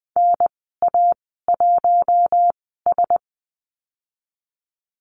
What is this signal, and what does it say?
Morse code, heard as a keyed tone, about 20 wpm, spaced as for 15 wpm, 700 Hz. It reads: NA1S
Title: 048 Lesson 9g - Number 4 - Call Signs 20wpm